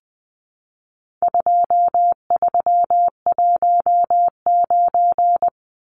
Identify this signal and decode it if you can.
Morse code, heard as a keyed tone, about 20 wpm, 700 Hz. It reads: 2319
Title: Numbers - 4-Digit 20wpm